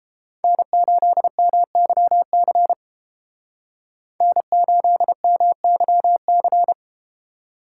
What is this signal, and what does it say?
Morse code, heard as a keyed tone, about 30 wpm, 700 Hz. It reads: N8MYC N8MYC